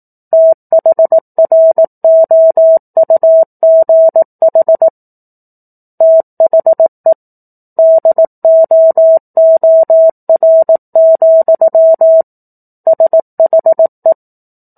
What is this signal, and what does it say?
Morse code, heard as a keyed tone, about 18 wpm, 650 Hz. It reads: THROUGH THE DOOR, SHE
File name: 04-Chapter-04 18 WPM